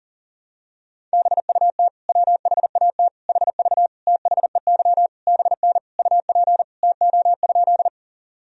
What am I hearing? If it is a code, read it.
Morse code, heard as a keyed tone, about 37 wpm, 700 Hz. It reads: BUT WHAT HV THEY BN UP TO?
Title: Sentences from Top 100 Words and Abbreviations 40wpm